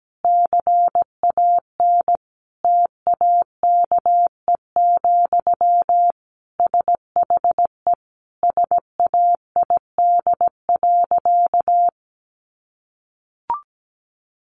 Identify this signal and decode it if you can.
Morse code, heard as a keyed tone, about 17 wpm, 700 Hz. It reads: CAN TAKE, SHE SAID.
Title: Sentences from Top 1500 Words 17wpm